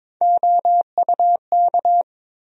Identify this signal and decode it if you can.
Morse code, heard as a keyed tone, about 22 wpm, 700 Hz. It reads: OUK
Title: USA Call Sign Postfixes - 3 Letters 22wpm